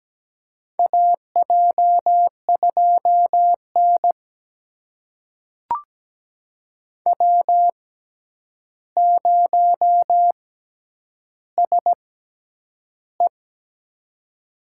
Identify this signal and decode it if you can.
Morse code, heard as a keyed tone, about 17 wpm, spaced as for 9 wpm, 700 Hz. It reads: AJ2N W0SE
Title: USA Call Signs - 4 Characters - 6x Character Spacing - 17wpm